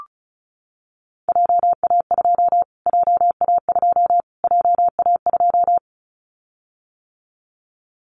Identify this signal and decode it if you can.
Morse code, heard as a keyed tone, about 34 wpm, 700 Hz. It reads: JA2 JA2 JA2
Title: Callsign Prefixes for Top 25 DXCC - Repeated 3x - 35wpm